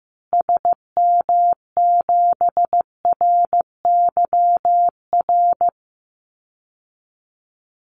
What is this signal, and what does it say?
Morse code, heard as a keyed tone, about 14 wpm, 700 Hz. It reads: SM7RYR
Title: Call Signs Repeated 2x - 15wpm